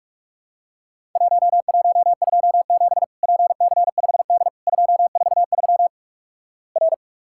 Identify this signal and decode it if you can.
Morse code, heard as a keyed tone, about 46 wpm, 700 Hz. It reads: 1127 PC5D 243 R
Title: Worked All Europe DX Contest - QTC Exchange - 45wpm